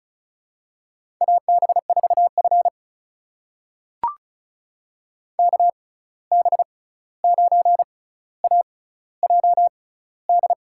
Morse code, heard as a keyed tone, about 36 wpm, spaced as for 16 wpm, 700 Hz. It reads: AB4F KB9AJD
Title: USA Call Signs - 6x Character Spacing - 35wpm